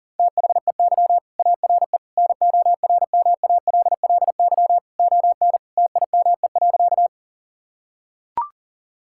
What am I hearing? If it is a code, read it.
Morse code, heard as a keyed tone, about 40 wpm, 700 Hz. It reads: THEY ARE NORMALLY ON TIME.
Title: Sentences from Top 1000 Words - Review Part 6 - 40wpm